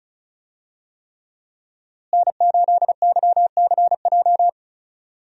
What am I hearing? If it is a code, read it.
Morse code, heard as a keyed tone, about 34 wpm, 700 Hz. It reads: N8YCJ
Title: USA Call Signs - 5 Characters - Repeated 3x - 35wpm